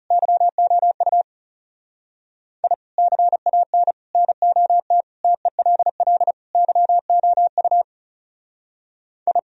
Morse code, heard as a keyed tone, about 37 wpm, 700 Hz. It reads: YOU I CAN NOT TELL YOU I
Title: Speed-Racing - Sets of 5 Words 30wpm